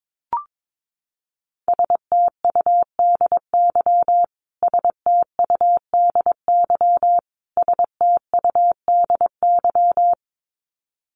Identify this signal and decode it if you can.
Morse code, heard as a keyed tone, about 22 wpm, 700 Hz. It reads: STUDY STUDY STUDY
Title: Top 300 Words Repeated 22wpm